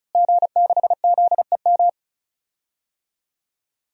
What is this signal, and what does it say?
Morse code, heard as a keyed tone, about 36 wpm, 700 Hz. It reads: G6ZEM
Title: DXCC-100 Call Signs - 5 Characters - Repeated 3x - 35wpm